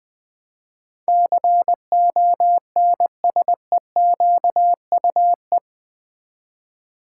Morse code, heard as a keyed tone, about 20 wpm, 700 Hz. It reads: CONSEQUE
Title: Build A Word - 4 Letters At A Time - 12-Letter Words - 20wpm